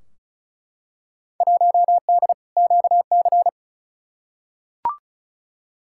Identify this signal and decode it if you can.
Morse code, heard as a keyed tone, about 36 wpm, 700 Hz. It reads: 1D QC
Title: Field Day Exchanges - 35wpm